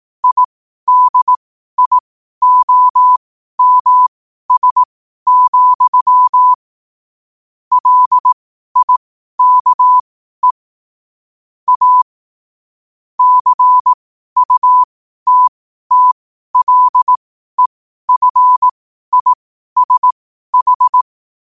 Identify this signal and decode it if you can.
Morse code, heard as a keyed tone, about 18 wpm, spaced as for 12 wpm, 1000 Hz. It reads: IDIOMS, LIKE A CUTTLEFISH
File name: qotdfiles / 12wpm